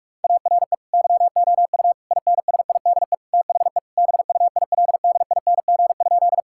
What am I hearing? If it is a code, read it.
Morse code, heard as a keyed tone, about 46 wpm, 700 Hz. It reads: ARE YOU INSIDE THE BUILDING?
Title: Sentences from Top 1000 Words - Review Part 6 - 45wpm